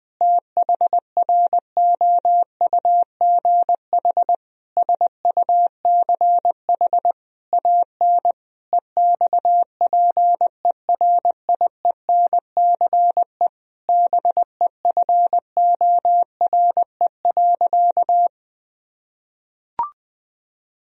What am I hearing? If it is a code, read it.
Morse code, heard as a keyed tone, about 21 wpm, 700 Hz. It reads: THROUGH SUCH AN EXPERIENCE BEFORE.
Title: Sentences from Top 1000 Words - Review Part 3 - 20wpm